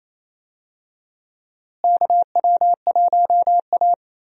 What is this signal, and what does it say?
Morse code, heard as a keyed tone, about 28 wpm, 700 Hz. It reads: KW1A
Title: USA Call Signs - 4 Characters - Repeated 2x - 28wpm